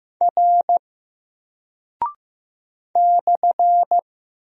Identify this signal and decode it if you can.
Morse code, heard as a keyed tone, about 15 wpm, 700 Hz. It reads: R /
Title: Single Character (Letter:Number:Punctuation:Prosign) Alt 15wpm